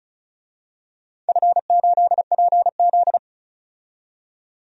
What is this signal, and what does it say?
Morse code, heard as a keyed tone, about 34 wpm, 700 Hz. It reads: F8PZ
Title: DXCC-100 Call Signs - 4 Characters - 35wpm